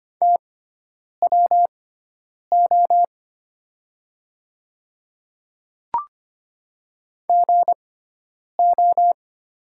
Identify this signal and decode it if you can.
Morse code, heard as a keyed tone, about 25 wpm, spaced as for 9 wpm, 690 Hz. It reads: TWO GO
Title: 6x Character Spacing - Top 100 Words - 25wpm